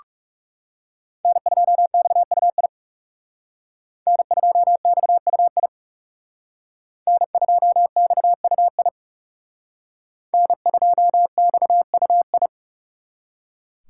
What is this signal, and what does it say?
Morse code, heard as a keyed tone, about 35 wpm, 700 Hz. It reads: N2XUI N2XUI N2XUI N2XUI
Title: Speed Racing - USA 1x3 Call Signs 30wpm